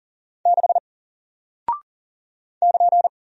Morse code, heard as a keyed tone, about 40 wpm, 700 Hz. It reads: 6 <KN>
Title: Single Character (Letter:Number:Punctuation:Prosign) Alt 40wpm